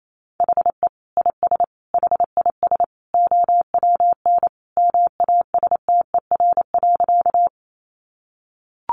Morse code, heard as a keyed tone, about 28 wpm, 700 Hz. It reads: HE IS HIS OWN MASTER.
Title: Sentences from Top 1300 Words 28wpm